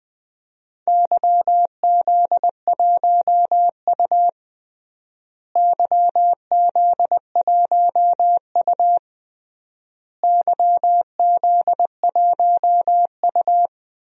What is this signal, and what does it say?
Morse code, heard as a keyed tone, about 20 wpm, 700 Hz. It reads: YZ1U YZ1U YZ1U
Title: DXCC-100 Call Signs - 4 Characters - Repeated 3x - 20wpm